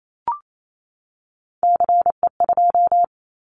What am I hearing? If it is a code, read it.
Morse code, heard as a keyed tone, about 28 wpm, 700 Hz. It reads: CE2
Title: Callsign Prefixes for Top 100 DXCC - 28wpm